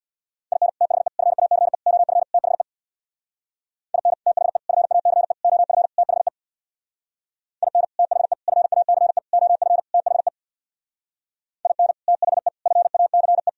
Morse code, heard as a keyed tone, about 64 wpm, 700 Hz. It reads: IN THE FACE OF THE IN THE FACE OF THE IN THE FACE OF THE IN THE FACE